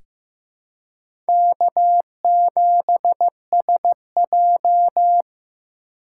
Morse code, heard as a keyed tone, about 15 wpm, 700 Hz. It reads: K7SJ
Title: USA Call Signs - 4 Characters - 4x Character Spacing - 15wpm